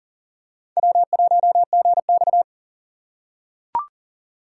Morse code, heard as a keyed tone, about 40 wpm, 700 Hz. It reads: W1GX
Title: USA Call Signs - 4 Characters - 4x Character Spacing - 40wpm